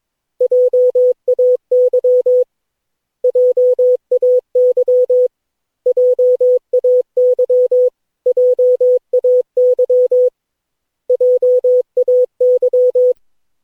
Morse code, heard as a keyed tone, about 22 wpm, 500 Hz. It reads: JAY JAY JAY JAY JAY
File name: cw / Session9 cw training